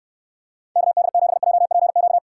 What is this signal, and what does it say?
Morse code, heard as a keyed tone, about 69 wpm, 700 Hz. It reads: KD7JPQ